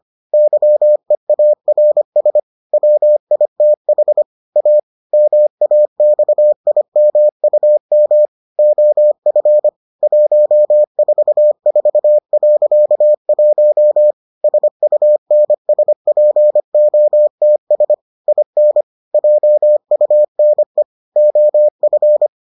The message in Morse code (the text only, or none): YEARS WITH A MAXIMUM OF 144.1 SUNSPOTS IN JUNE OF